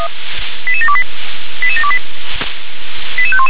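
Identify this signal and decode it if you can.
Five-tone selective call (ZVEI), heard as quick runs of five tones, tones of about 70 ms. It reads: 88728 88728 88511
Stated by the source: ZVEI five-tone call received by SDR.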